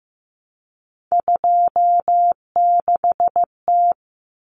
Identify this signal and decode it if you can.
Morse code, heard as a keyed tone, about 15 wpm, 700 Hz. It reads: 26T
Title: Common Cut Numbers - 3-Digit 15wpm